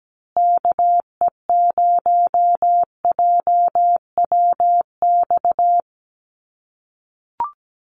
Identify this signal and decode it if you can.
Morse code, heard as a keyed tone, about 17 wpm, 700 Hz. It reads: KE0JWX
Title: USA Call Signs Repeated 2x 17wpm